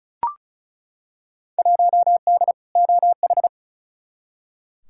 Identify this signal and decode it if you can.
Morse code, heard as a keyed tone, about 34 wpm, 700 Hz. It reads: 1D OH
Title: Field Day Exchanges - 35wpm